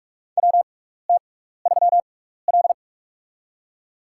Morse code, heard as a keyed tone, about 43 wpm, spaced as for 15 wpm, 700 Hz. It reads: WT3L
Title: USA Call Signs - 6x Character Spacing - 45wpm